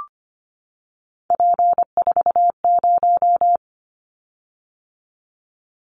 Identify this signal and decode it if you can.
Morse code, heard as a keyed tone, about 25 wpm, 700 Hz. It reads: P40
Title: Callsign Prefixes for Top 100 DXCC - 25wpm